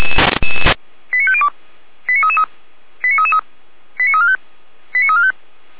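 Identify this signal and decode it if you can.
Five-tone selective call (ZVEI), heard as quick runs of five tones, tones of about 70 ms. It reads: 89502 89333 89333 89345 89345
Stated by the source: Five-tone ZVEI selective call 89280, received by SDR.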